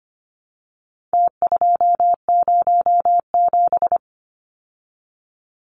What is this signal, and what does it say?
Morse code, heard as a keyed tone, about 25 wpm, 700 Hz. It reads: T207
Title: Numbers - Contest Style - 4-Digit 25wpm